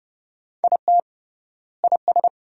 Morse code, heard as a keyed tone, about 30 wpm, 700 Hz. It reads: IT IS
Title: 3x Word-Spacing - Sentences from Top 200 Words - 30wpm